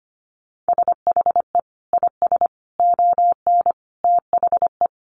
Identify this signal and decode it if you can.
Morse code, heard as a keyed tone, about 25 wpm, 700 Hz. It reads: SHE IS ON THE
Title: Sentences from Top 1200 Words 25wpm